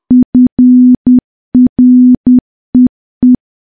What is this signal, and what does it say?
Morse code, heard as a keyed tone, about 10 wpm, 250 Hz. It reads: FREE
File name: morse / morse-training-10WPM